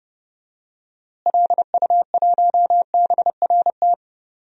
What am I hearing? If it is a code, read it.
Morse code, heard as a keyed tone, about 31 wpm, 700 Hz. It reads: LU1BRT
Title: DXCC-100 Call Signs - 6 Characters - Repeated 3x - 30wpm